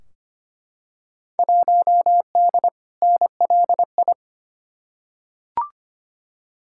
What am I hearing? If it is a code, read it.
Morse code, heard as a keyed tone, about 25 wpm, 700 Hz. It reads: 1D NLI
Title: Field Day Exchanges - 25wpm